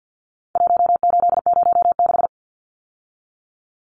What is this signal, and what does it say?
Morse code, heard as a keyed tone, about 50 wpm, 700 Hz. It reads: 1806
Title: Numbers - 4-Digit 50wpm